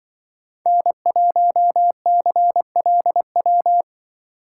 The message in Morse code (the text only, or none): N1CLW